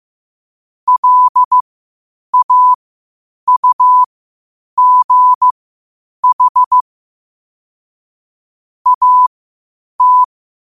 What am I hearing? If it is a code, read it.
Morse code, heard as a keyed tone, about 15 wpm, spaced as for 8 wpm, 1000 Hz. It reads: LAUGH AT
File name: qotdfiles / 8wpm